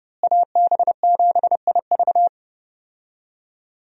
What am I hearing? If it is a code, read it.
Morse code, heard as a keyed tone, about 30 wpm, 700 Hz. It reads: AB7IV
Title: USA Call Signs - 5 Characters - Repeated 3x - 30wpm